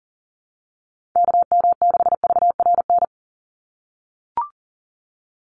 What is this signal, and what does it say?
Morse code, heard as a keyed tone, about 40 wpm, 700 Hz. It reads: KM6VRN
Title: USA Call Signs 40wpm